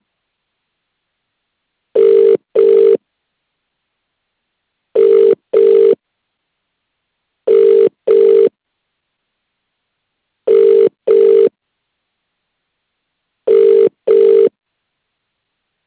Ringback tone (UK), 400 + 450 Hz.